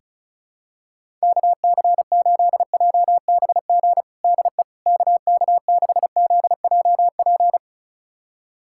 Morse code, heard as a keyed tone, about 36 wpm, 700 Hz. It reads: KC8JBG DE KK6ZJP
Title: USA Callsign DE Callsign - 6 Characters - 35wpm